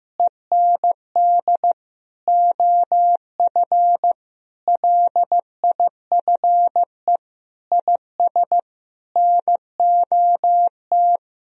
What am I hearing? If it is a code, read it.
Morse code, heard as a keyed tone, about 15 wpm, 700 Hz. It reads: END OF LIFE IS NOT
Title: Sentences from Top 1000 Words - Review Part 4 - 15wpm